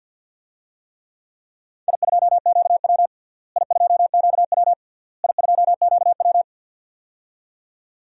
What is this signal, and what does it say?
Morse code, heard as a keyed tone, about 49 wpm, 700 Hz. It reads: I2QW I2QW I2QW
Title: DXCC-100 Call Signs - 4 Characters - Repeated 3x - 50wpm